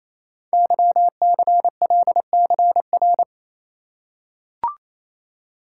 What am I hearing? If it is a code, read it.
Morse code, heard as a keyed tone, about 27 wpm, 700 Hz. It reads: YCLCR
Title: Code Groups - 5 Letters - 28wpm